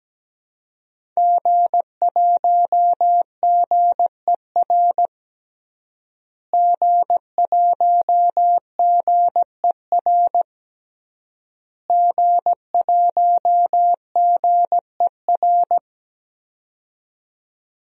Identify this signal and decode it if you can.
Morse code, heard as a keyed tone, about 17 wpm, 700 Hz. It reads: G1GER G1GER G1GER
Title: DXCC-100 Call Signs - 5 Characters - Repeated 3x - 17wpm